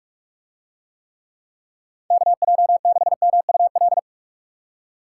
Morse code, heard as a keyed tone, about 41 wpm, 700 Hz. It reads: KJ6MUL